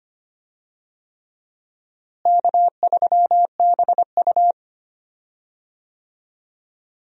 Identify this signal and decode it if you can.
Morse code, heard as a keyed tone, about 25 wpm, 700 Hz. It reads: K3BU
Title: USA Call Signs - 4 Characters - Repeated 3x - 25wpm